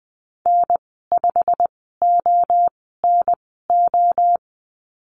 Morse code, heard as a keyed tone, about 20 wpm, spaced as for 15 wpm, 700 Hz. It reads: N5ONO
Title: 088 Lesson 15g - Number 5 - Call Signs 20wpm